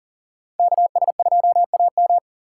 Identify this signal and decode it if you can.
Morse code, heard as a keyed tone, about 53 wpm, 700 Hz. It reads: KS2AM